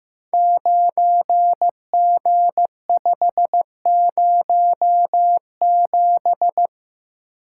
Morse code, heard as a keyed tone, about 15 wpm, 700 Hz. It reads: 9G507
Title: Code Groups - 5 Letter:Numbers - 15wpm